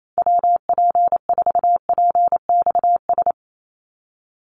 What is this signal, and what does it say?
Morse code, heard as a keyed tone, about 28 wpm, 700 Hz. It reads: WP4PXS
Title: USA Call Signs 28wpm